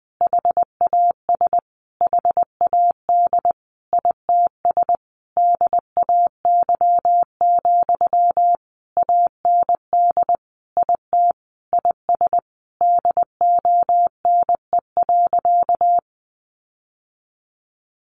Morse code, heard as a keyed tone, about 20 wpm, 700 Hz. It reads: HAS HAD ITS DAY, AND IT IS DONE.
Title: Sentences from Top 1500 Words 20wpm